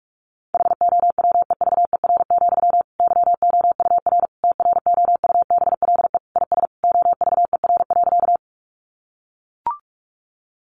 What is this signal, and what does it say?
Morse code, heard as a keyed tone, about 44 wpm, 700 Hz. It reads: HOWEVER, YOUR TROUBLE IS OVER.
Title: Sentences from Top 1200 Words 45wpm